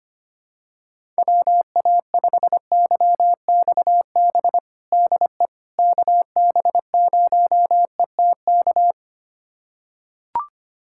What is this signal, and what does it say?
Morse code, heard as a keyed tone, about 25 wpm, 700 Hz. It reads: WA5YXB DE KB0ETK
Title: USA Callsign DE Callsign - 6 Characters - 25wpm